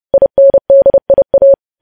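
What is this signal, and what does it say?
Morse code, heard as a keyed tone, about 30 wpm, 550 Hz. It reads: INDIA
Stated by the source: Morse news headlines sent at 30 wpm.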